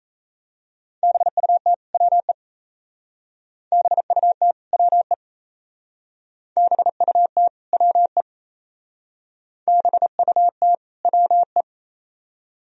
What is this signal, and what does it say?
Morse code, heard as a keyed tone, about 34 wpm, 700 Hz. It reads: BUT WE BUT WE BUT WE BUT WE
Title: Speed-Racing - Sets of 2 Words 28wpm